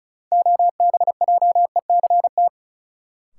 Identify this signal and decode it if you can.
Morse code, heard as a keyed tone, about 36 wpm, 700 Hz. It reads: OBJECT
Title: Sentences from Top 500 Words - New Word Drill - 35wpm